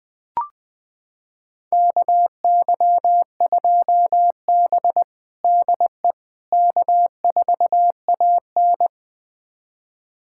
Morse code, heard as a keyed tone, about 20 wpm, 700 Hz. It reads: KY2B DE K4AN
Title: USA Callsign DE Callsign - 4 Characters - 20wpm